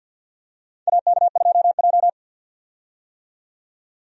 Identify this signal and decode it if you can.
Morse code, heard as a keyed tone, about 52 wpm, 700 Hz. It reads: AK2J
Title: USA Call Signs - 4 Characters - 50wpm